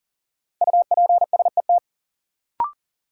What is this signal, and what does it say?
Morse code, heard as a keyed tone, about 40 wpm, 700 Hz. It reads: UPSET
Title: Sentences from Top 800 Words - New Word Drill - 40wpm